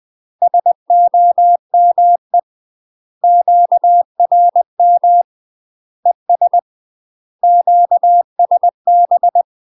Morse code, heard as a keyed tone, about 20 wpm, 700 Hz. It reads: SOME QRM ES QSB
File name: UK MASTER QSOs TRAINING / 20WPM / UK MASTER QSOs 0068